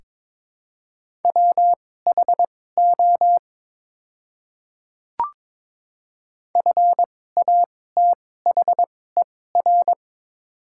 Morse code, heard as a keyed tone, about 22 wpm, spaced as for 16 wpm, 700 Hz. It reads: WHO FATHER
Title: 2x Character Spacing - Top 300 Words - 22wpm